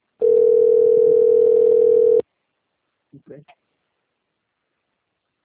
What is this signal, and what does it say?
Ringback tone (North America), 440 + 480 Hz, with speech.
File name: rg-1600-unknown- - - .75746